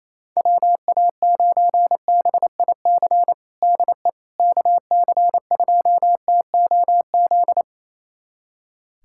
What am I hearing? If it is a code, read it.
Morse code, heard as a keyed tone, about 28 wpm, 700 Hz. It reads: WA9BIC DE KC2TOZ
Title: USA Callsign DE Callsign 28wpm